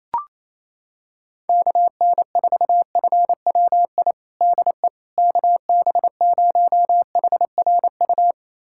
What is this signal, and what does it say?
Morse code, heard as a keyed tone, about 28 wpm, 700 Hz. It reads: KN4FWI DE KB0HRU
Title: USA Callsign DE Callsign - 6 Characters - 28wpm